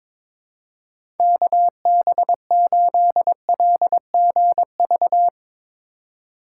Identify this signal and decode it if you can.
Morse code, heard as a keyed tone, about 22 wpm, 700 Hz. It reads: KB8LGV
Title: USA Call Signs - 6x Character Spacing - 22wpm